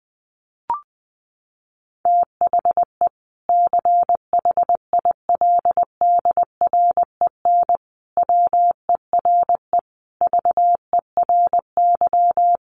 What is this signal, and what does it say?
Morse code, heard as a keyed tone, about 20 wpm, 700 Hz. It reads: THE CHILDREN WERE VERY
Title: Sentences from Top 1300 Words 20wpm